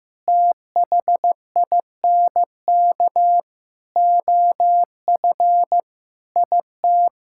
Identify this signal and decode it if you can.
Morse code, heard as a keyed tone, about 15 wpm, 700 Hz. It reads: THINK OF IT
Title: Sentences from Top 1000 Words - Review Part 5 - 15wpm